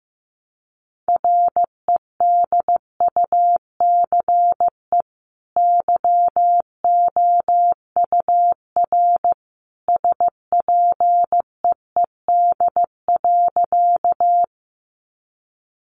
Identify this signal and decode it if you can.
Morse code, heard as a keyed tone, about 15 wpm, 700 Hz. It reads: REDUCE YOUR SPEED.
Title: Sentences from Top 1100 Words 15wpm